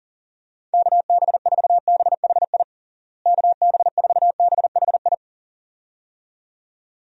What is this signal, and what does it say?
Morse code, heard as a keyed tone, about 40 wpm, 700 Hz. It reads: KB4BHI KB4BHI
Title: USA Call Signs - 6 Characters - Repeated 2x - 40wpm